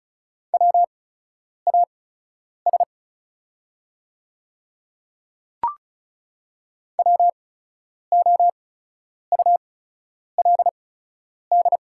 Morse code, heard as a keyed tone, about 36 wpm, spaced as for 10 wpm, 700 Hz. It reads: WAS WOULD
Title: 8x Character Spacing - Top 100 Words - 35wpm